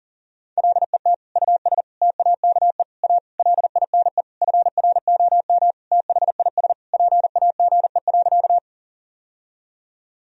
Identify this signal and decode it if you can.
Morse code, heard as a keyed tone, about 40 wpm, 700 Hz. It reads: LET US TAKE A LINE FROM THIS PAGE.
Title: Sentences from Top 1000 Words - Review Part 1 - 40wpm